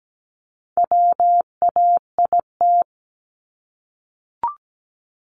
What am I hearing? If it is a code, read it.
Morse code, heard as a keyed tone, about 17 wpm, 700 Hz. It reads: WAIT
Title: ING Suffix - Top 100 Words with Base Words 17wpm